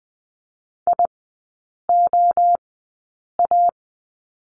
Morse code, heard as a keyed tone, about 20 wpm, spaced as for 8 wpm, 700 Hz. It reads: IOA
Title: 041 Lesson 8e - Number 1 - Three Characters 20wpm